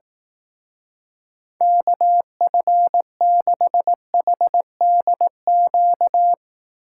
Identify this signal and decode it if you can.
Morse code, heard as a keyed tone, about 18 wpm, 700 Hz. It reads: KF6HDQ
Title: Speed Racing - USA 2x3 Call Signs 15wpm